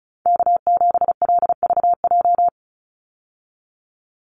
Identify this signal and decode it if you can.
Morse code, heard as a keyed tone, about 34 wpm, 700 Hz. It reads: K7LVJ
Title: Call Signs Repeated 2x - 35wpm